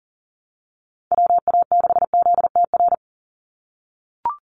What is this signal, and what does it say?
Morse code, heard as a keyed tone, about 40 wpm, 690 Hz. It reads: WA6ZTR
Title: USA Call Signs 40wpm